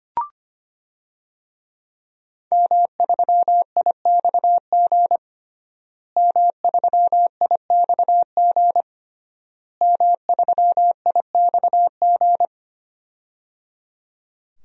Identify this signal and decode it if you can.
Morse code, heard as a keyed tone, about 25 wpm, 700 Hz. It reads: M3IXG M3IXG M3IXG
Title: DXCC-100 Call Signs - 5 Characters - Repeated 3x - 25wpm